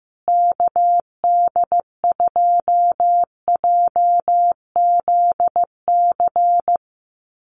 Morse code, heard as a keyed tone, about 15 wpm, 700 Hz. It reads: KD2JZC
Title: USA Call Signs Repeated 2x 15wpm